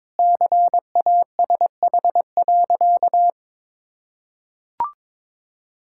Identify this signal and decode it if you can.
Morse code, heard as a keyed tone, about 22 wpm, 700 Hz. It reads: CASH.
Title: Sentences from Top 1000 Words - Review Part 5 - 22wpm